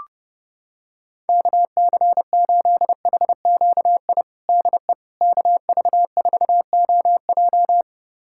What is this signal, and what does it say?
Morse code, heard as a keyed tone, about 30 wpm, 700 Hz. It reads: KC8HQI DE KV4OJ
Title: USA Callsign DE Callsign 30wpm